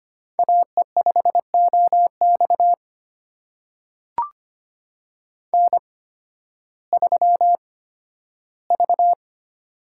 Morse code, heard as a keyed tone, about 25 wpm, spaced as for 13 wpm, 700 Hz. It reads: AE5OX N3V
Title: USA Call Signs - 8x Character Spacing - 25wpm